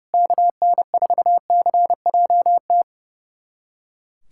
Morse code, heard as a keyed tone, about 30 wpm, 700 Hz. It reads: KN4CJT
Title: USA Callsign DE Callsign - 6 Characters - 30wpm